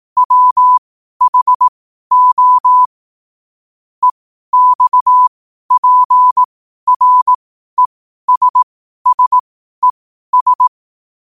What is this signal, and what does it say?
Morse code, heard as a keyed tone, about 18 wpm, spaced as for 12 wpm, 1000 Hz. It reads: WHO EXPRESSES
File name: 12wpm